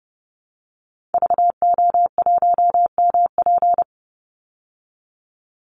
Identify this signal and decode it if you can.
Morse code, heard as a keyed tone, about 30 wpm, 700 Hz. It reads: VO1MP
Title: Call Signs Repeated 3x - 30wpm